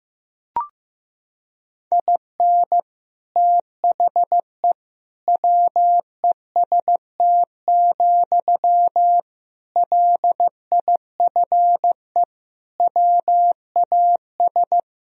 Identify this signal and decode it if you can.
Morse code, heard as a keyed tone, about 15 wpm, 700 Hz. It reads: IN THE WEST, LIFE WAS
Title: Sentences from Top 1000 Words - Review Part 2 - 15wpm